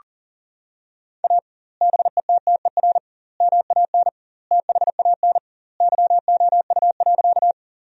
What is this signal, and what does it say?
Morse code, heard as a keyed tone, about 40 wpm, 700 Hz. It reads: A BETTER MAN THAN YOU.
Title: 2x Word-Spacing - Sentences from Top 400 Words - 40wpm